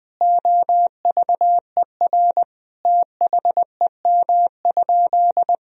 Morse code, heard as a keyed tone, about 20 wpm, 700 Hz. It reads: OVER THEM?
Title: Sentences from Top 1000 Words - Review Part 3 - 20wpm